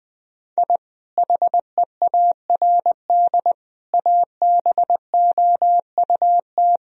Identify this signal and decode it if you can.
Morse code, heard as a keyed tone, about 20 wpm, 700 Hz. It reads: I HEARD ABOUT
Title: Sentences from Top 1000 Words - Review Part 1 - 20wpm